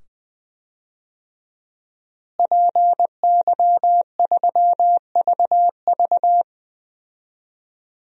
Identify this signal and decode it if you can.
Morse code, heard as a keyed tone, about 20 wpm, 700 Hz. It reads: PY3VV
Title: DXCC-100 Call Signs - 5 Characters - Repeated 2x - 20wpm